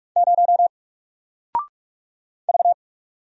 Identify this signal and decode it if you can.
Morse code, heard as a keyed tone, about 46 wpm, 700 Hz. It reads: 0 V
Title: Single Character (Letter:Number:Punctuation:Prosign) Alt 45wpm